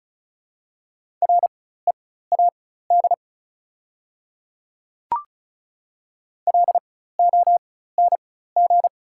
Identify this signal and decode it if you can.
Morse code, heard as a keyed tone, about 34 wpm, spaced as for 16 wpm, 690 Hz. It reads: READ LONG
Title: 4x Character Spacing - Top 200 Words - 35wpm